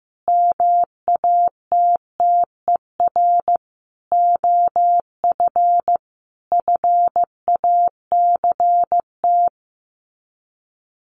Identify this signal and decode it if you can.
Morse code, heard as a keyed tone, about 15 wpm, 700 Hz. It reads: MATTER OF FACT
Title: Sets of 4 Words 15wpm